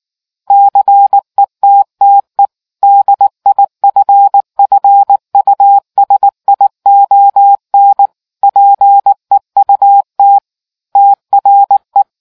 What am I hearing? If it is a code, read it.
Morse code, heard as a keyed tone, about 19 wpm, 800 Hz. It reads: CETTE DIFFUSION PEUT TRE